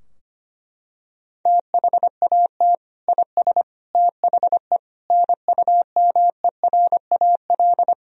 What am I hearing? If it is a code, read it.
Morse code, heard as a keyed tone, about 25 wpm, 700 Hz. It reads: THAT IS THE NUMERAL
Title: Sentences from Top 1000 Words - Review Part 2 - 25wpm